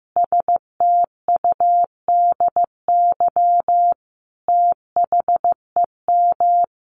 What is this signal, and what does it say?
Morse code, heard as a keyed tone, about 15 wpm, 700 Hz. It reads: STUDY THEM
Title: Sentences from Top 300 Words 15wpm